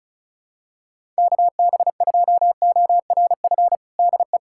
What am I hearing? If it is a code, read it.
Morse code, heard as a keyed tone, about 36 wpm, 700 Hz. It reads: KB2ORF DE
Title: USA Callsign DE Callsign - 6 Characters - 35wpm